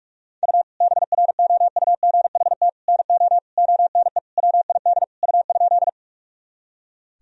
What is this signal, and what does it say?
Morse code, heard as a keyed tone, about 46 wpm, 700 Hz. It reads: U BROUGHT NO ONE WID U?
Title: Sentences from Top 500 Words and Abbreviations 45wpm